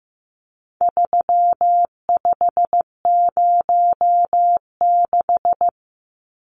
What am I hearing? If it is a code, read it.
Morse code, heard as a keyed tone, about 15 wpm, 700 Hz. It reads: 3506
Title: Numbers - 4-Digit 15wpm